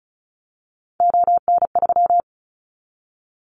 Morse code, heard as a keyed tone, about 36 wpm, 690 Hz. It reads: ON3
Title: Callsign Prefixes for Top 50 DXCC - 35wpm